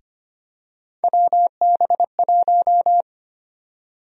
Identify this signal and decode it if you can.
Morse code, heard as a keyed tone, about 25 wpm, 700 Hz. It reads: WB1
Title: USA Call Sign Prefixes - Repeated 2x - 25wpm